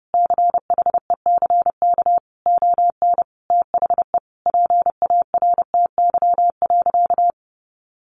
Morse code, heard as a keyed tone, about 30 wpm, 700 Hz. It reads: CHECK ON THE PARTY.
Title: Sentences from Top 1400 Words 30wpm